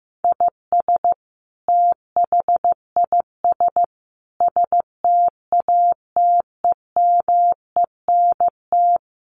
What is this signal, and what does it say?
Morse code, heard as a keyed tone, about 15 wpm, 700 Hz. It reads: IS THIS STATEMENT
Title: Sentences from Top 1200 Words 15wpm